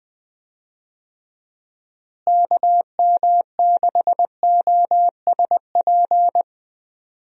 Morse code, heard as a keyed tone, about 21 wpm, 700 Hz. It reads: KM6OSP
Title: USA Call Signs - 6 Characters - Repeated 2x - 20wpm